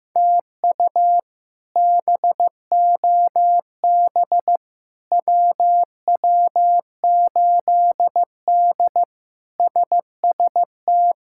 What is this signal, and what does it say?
Morse code, heard as a keyed tone, about 15 wpm, 700 Hz. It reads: TU BOB WW8D SST
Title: K1USN SST QSO Practice - 15wpm